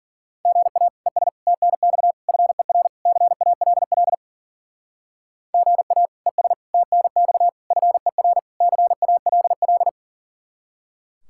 Morse code, heard as a keyed tone, about 52 wpm, 700 Hz. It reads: GA ES TNX FER CALL GA ES TNX FER CALL